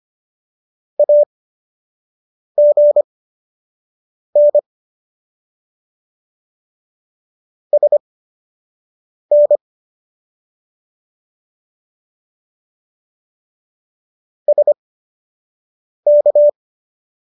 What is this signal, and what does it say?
Morse code, heard as a keyed tone, about 25 wpm, spaced as for 6 wpm, 590 Hz. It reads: AGN SN SK